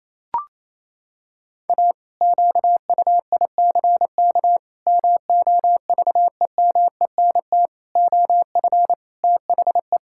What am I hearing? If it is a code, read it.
Morse code, heard as a keyed tone, about 27 wpm, 700 Hz. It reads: A QUICK MOVEMENT OF THE
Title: Pangram Challenge v2 - 28wpm